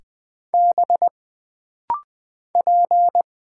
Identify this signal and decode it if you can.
Morse code, heard as a keyed tone, about 20 wpm, 700 Hz. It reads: B P
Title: Single Letters - Rapid-Fire 20wpm